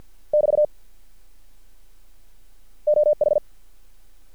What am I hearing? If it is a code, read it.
Morse code, heard as a keyed tone, about 50 wpm, 600 Hz. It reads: <BT> OH
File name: archived / CRSnet